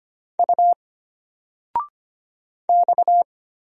Morse code, heard as a keyed tone, about 25 wpm, 700 Hz. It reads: U X
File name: Single-Letter-Number-25wpm